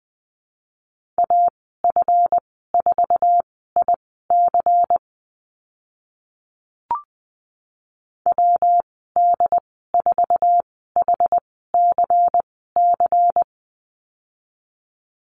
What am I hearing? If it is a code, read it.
Morse code, heard as a keyed tone, about 20 wpm, spaced as for 14 wpm, 700 Hz. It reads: AF4IC WD4HCC
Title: 161 Lesson 26g - Letter g - Call Signs 20wpm